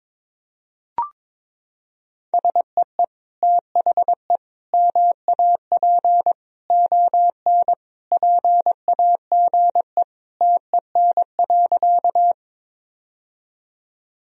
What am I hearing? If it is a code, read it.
Morse code, heard as a keyed tone, about 22 wpm, 700 Hz. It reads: SEE THE MAP ON PAGE TEN.
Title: Sentences from Top 1000 Words - Review Part 2 - 22wpm